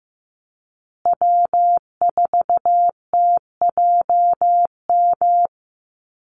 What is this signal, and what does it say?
Morse code, heard as a keyed tone, about 15 wpm, 700 Hz. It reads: W4TJM
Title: USA Call Signs Repeated 2x 15wpm